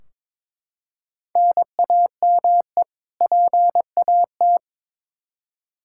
Morse code, heard as a keyed tone, about 22 wpm, 700 Hz. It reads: NAME PAT
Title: Practice QSO - Exchange 1 - Beginning - 22wpm